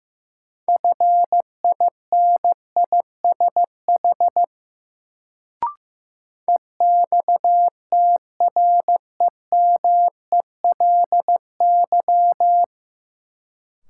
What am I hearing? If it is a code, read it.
Morse code, heard as a keyed tone, about 15 wpm, 700 Hz. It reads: FINISH EXTREMELY
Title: Sentences from Top 1200 Words - New Word Drill Alt - 15wpm